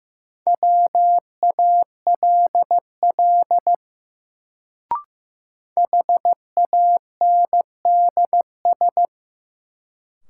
Morse code, heard as a keyed tone, about 15 wpm, 700 Hz. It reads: WALL HANDS
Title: Sentences from Top 1000 Words - Review Part 6 - New Word Drill - 15wpm